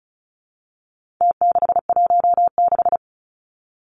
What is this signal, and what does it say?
Morse code, heard as a keyed tone, about 34 wpm, 700 Hz. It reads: T616
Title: Numbers - Contest Style - 4-Digit 35wpm